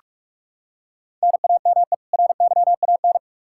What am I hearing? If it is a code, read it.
Morse code, heard as a keyed tone, about 46 wpm, 700 Hz. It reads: NAME RYAN
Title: Practice QSO - Exchange 1 - Beginning - 45wpm